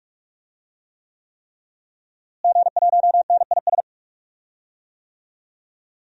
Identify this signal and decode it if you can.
Morse code, heard as a keyed tone, about 46 wpm, 700 Hz. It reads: G1NIS
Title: DXCC-100 Call Signs - 5 Characters - Repeated 3x - 45wpm